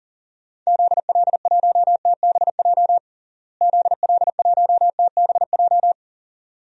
Morse code, heard as a keyed tone, about 40 wpm, 700 Hz. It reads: ZL1TBJ ZL1TBJ
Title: DXCC-100 Call Signs - 6 Characters - Repeated 2x - 40wpm